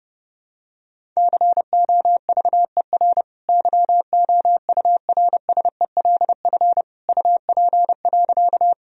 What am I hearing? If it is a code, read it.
Morse code, heard as a keyed tone, about 30 wpm, 700 Hz. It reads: COVER YOURSELF UP.
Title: Sentences from Top 1000 Words - Review Part 1 - 30wpm